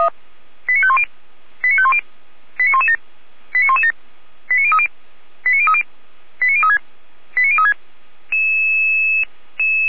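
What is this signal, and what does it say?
Five-tone selective call (ZVEI), heard as quick runs of five tones, tones of about 70 ms. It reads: 89510 89510 89117 89117 89030 89030 89036 89036
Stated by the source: ZVEI five-tone call 89105 received by SDR.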